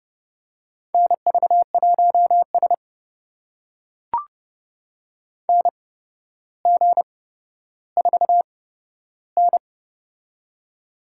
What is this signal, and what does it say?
Morse code, heard as a keyed tone, about 32 wpm, spaced as for 13 wpm, 700 Hz. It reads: NV1S NG4N
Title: USA Call Signs - 4 Characters - 8x Character Spacing - 30wpm